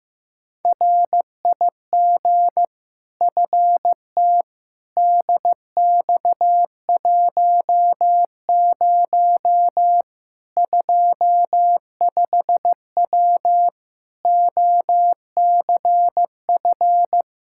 Morse code, heard as a keyed tone, about 15 wpm, 700 Hz. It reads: RIG FT DX10 25W OCF
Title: Practice QSO - Exchange 2 - Info - 15wpm